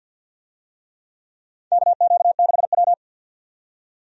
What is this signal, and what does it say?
Morse code, heard as a keyed tone, about 53 wpm, 700 Hz. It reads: KQ6W
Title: USA Call Signs - 4 Characters - 50wpm